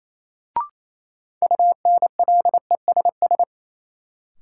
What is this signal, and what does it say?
Morse code, heard as a keyed tone, about 28 wpm, 700 Hz. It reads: UNLESS
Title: Sentences from Top 1000 Words - Review Part 5 - New Word Drill - 28wpm